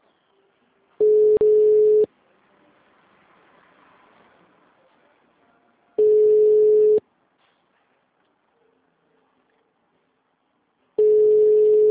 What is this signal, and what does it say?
Ringback tone (Europe, 425 Hz).
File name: out- -2013- - - .782